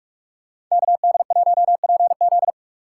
Morse code, heard as a keyed tone, about 44 wpm, 700 Hz. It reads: KD1PZ